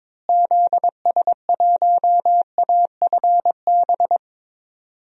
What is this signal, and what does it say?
Morse code, heard as a keyed tone, about 22 wpm, 700 Hz. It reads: ZS1AFB
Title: DXCC-100 Call Signs - 6 Characters - Repeated 3x - 22wpm